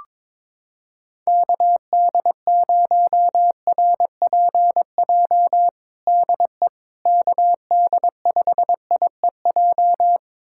Morse code, heard as a keyed tone, about 21 wpm, 700 Hz. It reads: KD0RPJ DE KD5IEJ
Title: USA Callsign DE Callsign - 6 Characters - 22wpm